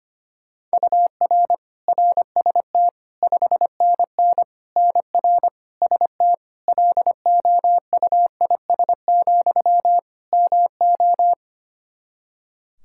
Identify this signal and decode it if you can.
Morse code, heard as a keyed tone, about 25 wpm, 700 Hz. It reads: UR RST 5NN NR ST LOUIS, MO
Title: Practice QSO - Exchange 1 - Beginning - 25wpm